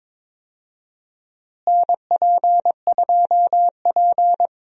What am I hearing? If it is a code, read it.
Morse code, heard as a keyed tone, about 21 wpm, 700 Hz. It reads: NP2P
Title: USA Call Signs - 4 Characters - Repeated 3x - 22wpm